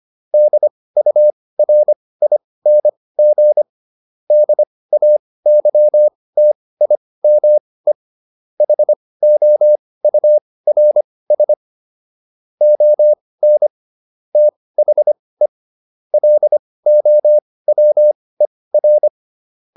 Morse code, heard as a keyed tone, about 25 wpm, spaced as for 18 wpm, 600 Hz. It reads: DURING DAYTIME HOURS ON THE LOWER
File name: SS211 18